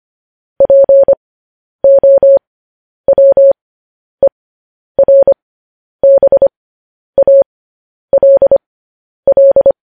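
Morse code, heard as a keyed tone, about 25 wpm, spaced as for 10 wpm, 550 Hz. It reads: POWERBALL
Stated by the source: Morse code news headlines.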